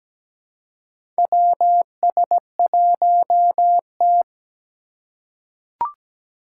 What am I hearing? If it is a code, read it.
Morse code, heard as a keyed tone, about 17 wpm, 700 Hz. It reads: WS1T
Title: USA Call Signs - 4 Characters - 4x Character Spacing - 17wpm